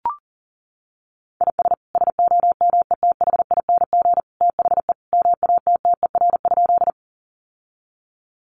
Morse code, heard as a keyed tone, about 40 wpm, 700 Hz. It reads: IS SOMETHING THE MATTER?
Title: Sentences from Top 700 Words 40wpm